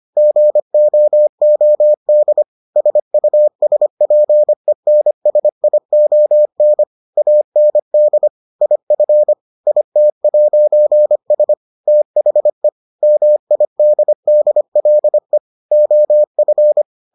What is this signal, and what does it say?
Morse code, heard as a keyed tone, about 25 wpm, 600 Hz. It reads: GOOD SUSPENSION AND IF IT'S THE MIDDLE OF